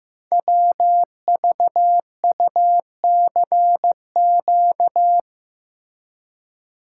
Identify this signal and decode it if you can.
Morse code, heard as a keyed tone, about 15 wpm, 700 Hz. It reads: WVUCQ
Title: Code Groups - 5 Letters - 15wpm